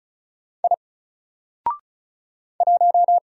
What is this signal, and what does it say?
Morse code, heard as a keyed tone, about 36 wpm, 700 Hz. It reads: I 1
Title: Single Letter-Number Alt 35wpm